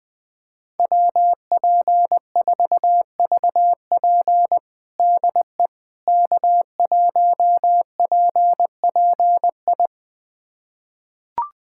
Morse code, heard as a keyed tone, about 21 wpm, 700 Hz. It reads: WP4VP DE K1PPI
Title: USA Callsign DE Callsign - 5 Characters - 20wpm